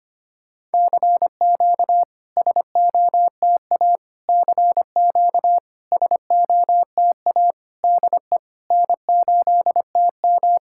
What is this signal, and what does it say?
Morse code, heard as a keyed tone, about 25 wpm, 700 Hz. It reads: CQ SOTA CQ SOTA DE N8TM
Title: SOTA QSO Practice 25wpm